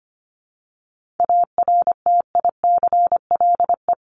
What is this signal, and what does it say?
Morse code, heard as a keyed tone, about 25 wpm, 700 Hz. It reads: ARTICLE
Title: Top 751-1000 Words 25wpm